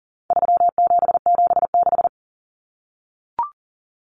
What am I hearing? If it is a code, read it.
Morse code, heard as a keyed tone, about 40 wpm, 700 Hz. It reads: 3776
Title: Numbers - 4-Digit 40wpm